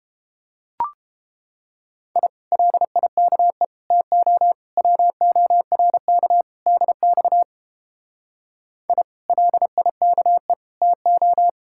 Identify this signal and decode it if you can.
Morse code, heard as a keyed tone, about 29 wpm, 700 Hz. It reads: I LIKE TO WORK DX I LIKE TO